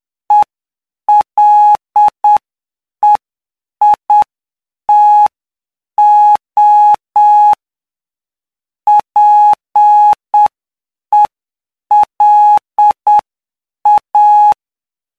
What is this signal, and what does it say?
Morse code, heard as a keyed tone, about 7 wpm, 840 Hz. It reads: ELEITO PELA
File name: cw / py2lcd / 15-PY2LCD-Li��o 24